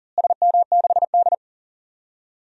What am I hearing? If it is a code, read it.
Morse code, heard as a keyed tone, about 39 wpm, 700 Hz. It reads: SM6D
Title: DXCC-100 Call Signs - 4 Characters - Repeated 2x - 40wpm